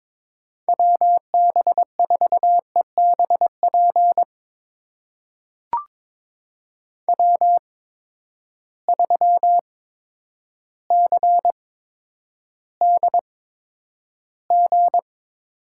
Morse code, heard as a keyed tone, about 22 wpm, spaced as for 10 wpm, 700 Hz. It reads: WB4EBP W3CDG
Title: USA Call Signs - 8x Character Spacing - 22wpm